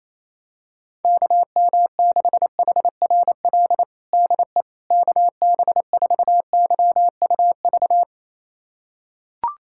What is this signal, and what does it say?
Morse code, heard as a keyed tone, about 28 wpm, 700 Hz. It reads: KM6HRL DE KB4YUV
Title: USA Callsign DE Callsign 28wpm